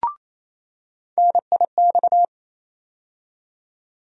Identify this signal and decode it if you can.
Morse code, heard as a keyed tone, about 28 wpm, 700 Hz. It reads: NIX
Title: USA Call Sign Postfixes - 3 Letters 28wpm